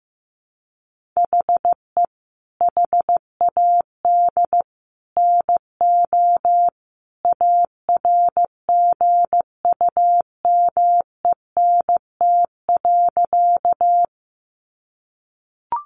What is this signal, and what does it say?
Morse code, heard as a keyed tone, about 15 wpm, 700 Hz. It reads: HE HAD NO ARGUMENT.
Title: Sentences from Top 1400 Words 15wpm